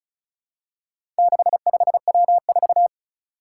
Morse code, heard as a keyed tone, about 36 wpm, 700 Hz. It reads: 65W4
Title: Four Challenging Letter-Number Combinations 35wpm